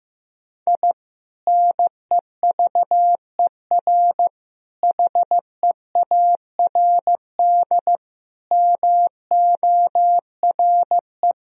Morse code, heard as a keyed tone, about 15 wpm, 700 Hz. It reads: I NEVER HEARD MORE
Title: Sentences from Top 1000 Words - Review Part 1 - 15wpm